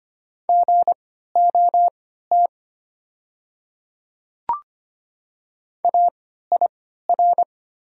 Morse code, heard as a keyed tone, about 25 wpm, spaced as for 14 wpm, 700 Hz. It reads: GOT AIR
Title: 3x Character Spacing - Top 300 Words - 25wpm